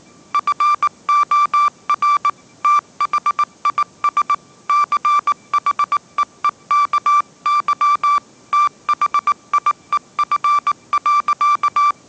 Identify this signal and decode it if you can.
Morse code, heard as a keyed tone, about 19 wpm, 1170 Hz. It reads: FOR THIS CHEEKY THIEF.
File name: 20wpm